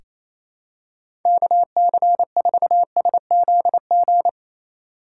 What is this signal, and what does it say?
Morse code, heard as a keyed tone, about 28 wpm, 700 Hz. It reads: KC4SZG
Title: USA Call Signs - 6 Characters - 4x Character Spacing - 28wpm